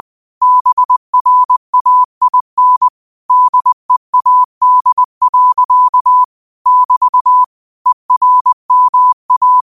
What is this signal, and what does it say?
Morse code, heard as a keyed tone, about 20 wpm, 1000 Hz. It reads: BRAIN DEAD. <BT> ERMA
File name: qotdfiles / 20wpm